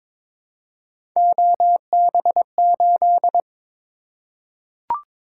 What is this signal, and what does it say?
Morse code, heard as a keyed tone, about 22 wpm, 700 Hz. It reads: OB8
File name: practice / 22wpm / Three Letter-Number Combinations - 22wpm